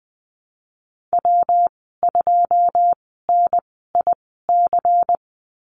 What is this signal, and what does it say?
Morse code, heard as a keyed tone, about 20 wpm, spaced as for 15 wpm, 700 Hz. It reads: W2NIC
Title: 186 Lesson 30g - Letter b - Call Signs 20wpm